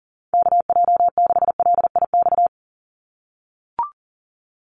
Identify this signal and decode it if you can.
Morse code, heard as a keyed tone, about 40 wpm, 700 Hz. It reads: KJ6LIX
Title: USA Call Signs 40wpm